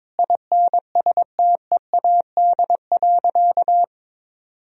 Morse code, heard as a keyed tone, about 22 wpm, 700 Hz. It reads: INSTEAD.
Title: Sentences from Top 1000 Words - Review Part 4 - 22wpm